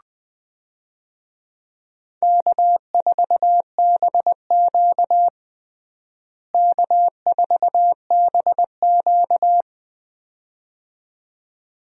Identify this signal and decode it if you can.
Morse code, heard as a keyed tone, about 20 wpm, 700 Hz. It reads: K4BQ K4BQ
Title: USA Call Signs - 4 Characters - Repeated 2x - 20wpm